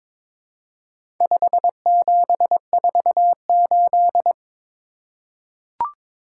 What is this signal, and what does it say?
Morse code, heard as a keyed tone, about 22 wpm, 700 Hz. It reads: 5748
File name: practice / 22wpm / Four Letter-Number Combinations - 22wpm